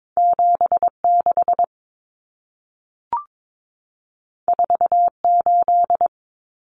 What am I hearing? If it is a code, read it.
Morse code, heard as a keyed tone, about 22 wpm, 700 Hz. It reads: 76 48
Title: Common Cut Numbers - 2-Digit 22wpm